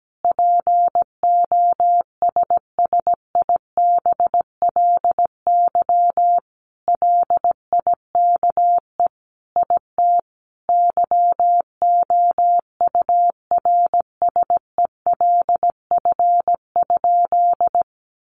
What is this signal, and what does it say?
Morse code, heard as a keyed tone, about 17 wpm, 700 Hz. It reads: POSSIBLY LIKE IT YOURSELF?
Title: Sentences from Top 1300 Words 17wpm